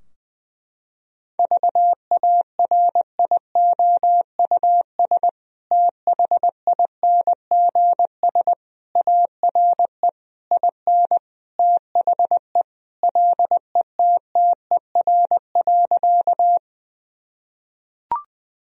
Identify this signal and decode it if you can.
Morse code, heard as a keyed tone, about 21 wpm, 700 Hz. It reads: VARIOUS THINGS ARE IN THE LETTER.
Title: Sentences from Top 1000 Words - Review Part 4 - 20wpm